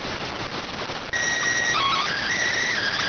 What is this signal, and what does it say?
SSTV (slow-scan television), PD 290.